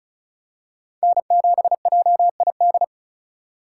Morse code, heard as a keyed tone, about 36 wpm, 700 Hz. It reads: N7JID
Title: USA Call Signs - 4x Character Spacing - 35wpm